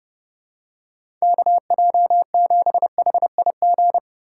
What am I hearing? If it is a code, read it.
Morse code, heard as a keyed tone, about 30 wpm, 700 Hz. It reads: KJ7HIG
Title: USA Call Signs - 6 Characters - 3x Character Spacing - 30wpm